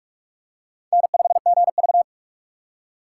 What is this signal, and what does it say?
Morse code, heard as a keyed tone, about 42 wpm, 690 Hz. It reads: N5GV
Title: USA Call Signs - 6x Character Spacing - 45wpm